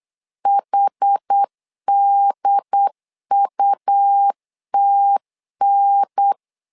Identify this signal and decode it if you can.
Morse code, heard as a keyed tone, about 8 wpm, 800 Hz. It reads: HDUTN
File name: 4 Q 8wpm